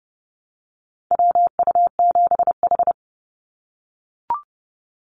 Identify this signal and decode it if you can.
Morse code, heard as a keyed tone, about 30 wpm, 700 Hz. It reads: WU7H
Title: NA SOTA Activator Callsigns 30wpm